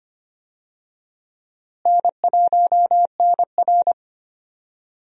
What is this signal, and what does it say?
Morse code, heard as a keyed tone, about 25 wpm, 700 Hz. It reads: N1NR
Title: USA Call Signs - 4 Characters - Repeated 2x - 25wpm